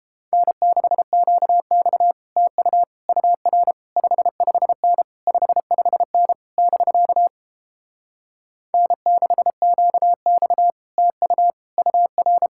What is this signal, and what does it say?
Morse code, heard as a keyed tone, about 30 wpm, 700 Hz. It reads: N6QX TU UR 55N 55N <BK> N6QX TU UR